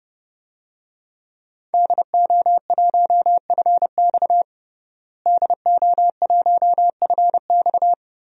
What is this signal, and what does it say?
Morse code, heard as a keyed tone, about 30 wpm, 700 Hz. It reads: DO1FX DO1FX
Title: DXCC-100 Call Signs - 5 Characters - Repeated 3x - 30wpm